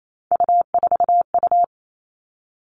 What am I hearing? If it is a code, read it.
Morse code, heard as a keyed tone, about 28 wpm, 700 Hz. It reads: U4U
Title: Three Challenging Letter-Number Combinations 28wpm